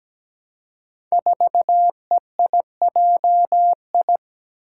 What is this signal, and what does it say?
Morse code, heard as a keyed tone, about 17 wpm, 700 Hz. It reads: 4EIJI
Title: Five Challenging Letter-Number Combinations 17wpm